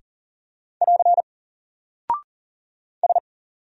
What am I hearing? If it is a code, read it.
Morse code, heard as a keyed tone, about 40 wpm, 700 Hz. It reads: <AR> S
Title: Single Character (Letter:Number:Punctuation:Prosign) 40wpm